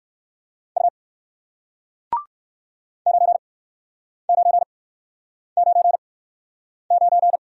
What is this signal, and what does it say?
Morse code, heard as a keyed tone, about 56 wpm, spaced as for 9 wpm, 700 Hz. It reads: U 9999
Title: Speed-Racing Single Letter-Number 45wpm